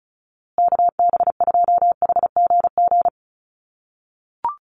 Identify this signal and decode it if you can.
Morse code, heard as a keyed tone, about 34 wpm, 700 Hz. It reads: KB2HGG
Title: USA Call Signs Repeated 2x 35wpm